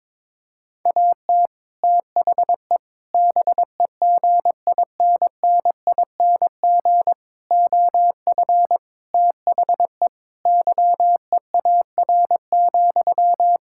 Morse code, heard as a keyed tone, about 22 wpm, 700 Hz. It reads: AT THE BEGINNING OF THE YEAR,
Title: Sentences from Top 1000 Words - Review Part 5 - 22wpm